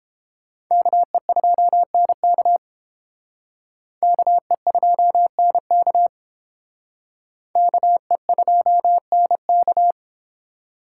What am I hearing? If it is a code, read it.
Morse code, heard as a keyed tone, about 31 wpm, 700 Hz. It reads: KE2NK KE2NK KE2NK